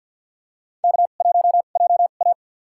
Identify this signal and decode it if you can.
Morse code, heard as a keyed tone, about 51 wpm, spaced as for 36 wpm, 700 Hz. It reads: K1JA
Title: USA Call Signs - 4 Characters - 2x Character Spacing - 50wpm